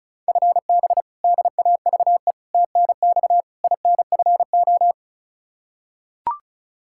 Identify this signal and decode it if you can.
Morse code, heard as a keyed tone, about 36 wpm, 700 Hz. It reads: FB DAVE TNX INFO
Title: K1USN SST QSO Practice 35wpm